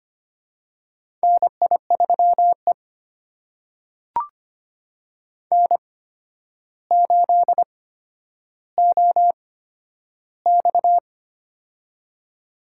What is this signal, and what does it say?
Morse code, heard as a keyed tone, about 25 wpm, spaced as for 11 wpm, 700 Hz. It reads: NI3E N8OX
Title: USA Call Signs - 4 Characters - 8x Character Spacing - 25wpm